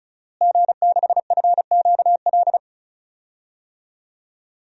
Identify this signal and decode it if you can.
Morse code, heard as a keyed tone, about 34 wpm, 700 Hz. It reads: G6FQL
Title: DXCC-100 Call Signs - 5 Characters - Repeated 2x - 35wpm